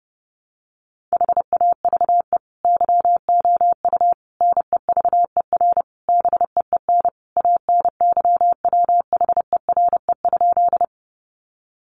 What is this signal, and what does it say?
Morse code, heard as a keyed tone, about 30 wpm, 700 Hz. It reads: HAVE YOU NEVER BEEN ANYWHERE?
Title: Sentences from Top 1400 Words 30wpm